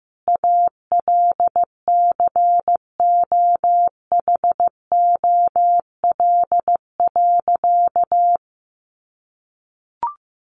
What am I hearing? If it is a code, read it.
Morse code, heard as a keyed tone, about 15 wpm, 700 Hz. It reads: ALCOHOL.
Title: Sentences from Top 1400 Words 15wpm